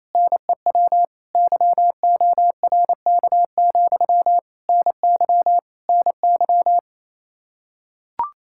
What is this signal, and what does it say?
Morse code, heard as a keyed tone, about 28 wpm, 700 Hz. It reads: NEW YORK, NY NY
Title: Practice QSO - Exchange 1 - Beginning - 28wpm